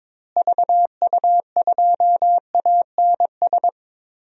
Morse code, heard as a keyed tone, about 22 wpm, 700 Hz. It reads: VU2ANS
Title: DXCC-100 Call Signs - 6 Characters - Repeated 3x - 22wpm